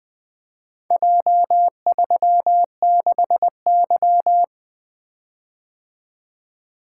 Morse code, heard as a keyed tone, about 24 wpm, 700 Hz. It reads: J36Y
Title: Speed-Racing - Four Letter-Number Combinations - 20wpm